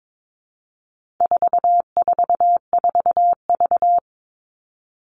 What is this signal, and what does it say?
Morse code, heard as a keyed tone, about 22 wpm, 700 Hz. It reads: 444V
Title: All Cut Numbers - 4-Digit 22wpm